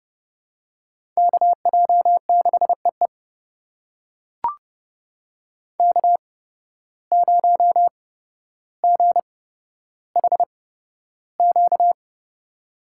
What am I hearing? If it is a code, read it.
Morse code, heard as a keyed tone, about 31 wpm, spaced as for 13 wpm, 700 Hz. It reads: KJ6EE K0GHQ
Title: USA Call Signs - 5 Characters - 8x Character Spacing - 30wpm